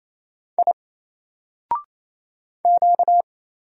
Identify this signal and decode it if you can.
Morse code, heard as a keyed tone, about 27 wpm, 700 Hz. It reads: I Q
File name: Single-Letter-Number-28wpm